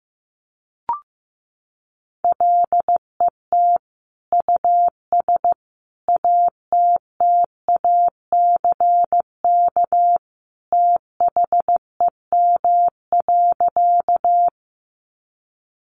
Morse code, heard as a keyed tone, about 15 wpm, 700 Hz. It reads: LET US ATTACK THEM.
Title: Sentences from Top 1500 Words 15wpm